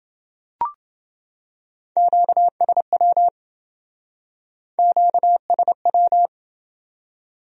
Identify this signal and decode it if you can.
Morse code, heard as a keyed tone, about 28 wpm, 700 Hz. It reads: QSW QSW